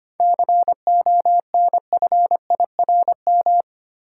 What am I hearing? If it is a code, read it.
Morse code, heard as a keyed tone, about 25 wpm, 700 Hz. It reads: CONFIRM
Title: Seven Letter Words - 25wpm